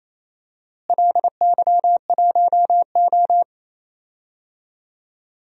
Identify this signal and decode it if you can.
Morse code, heard as a keyed tone, about 28 wpm, 700 Hz. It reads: LY1O
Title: DXCC-100 Call Signs - 4 Characters - Repeated 3x - 28wpm